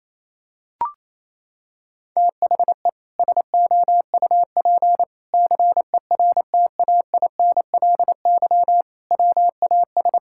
Morse code, heard as a keyed tone, about 27 wpm, 700 Hz. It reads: THE SOUP CERTAINLY WAS
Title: Sentences from Top 1000 Words - Review Part 6 - 28wpm